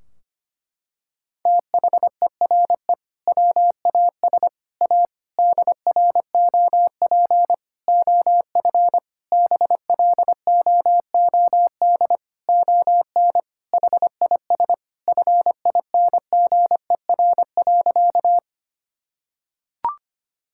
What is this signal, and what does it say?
Morse code, heard as a keyed tone, about 25 wpm, 700 Hz. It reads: THERE WAS A DROP OF BLOOD ON HIS FINGER.
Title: Sentences from Top 1600 Words 25wpm